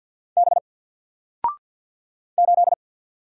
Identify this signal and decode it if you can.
Morse code, heard as a keyed tone, about 48 wpm, 700 Hz. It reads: B 8
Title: Single Character (Letter:Number:Punctuation:Prosign) Alt 50wpm